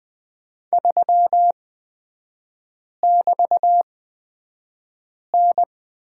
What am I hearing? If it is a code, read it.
Morse code, heard as a keyed tone, about 20 wpm, spaced as for 5 wpm, 700 Hz. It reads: 3<BT>N
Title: 224 Lesson 36e - BT - Three Characters 20wpm